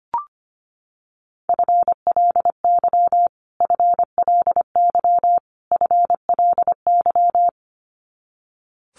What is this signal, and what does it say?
Morse code, heard as a keyed tone, about 25 wpm, 700 Hz. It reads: FLY FLY FLY
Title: Top 500 Words Repeated 25wpm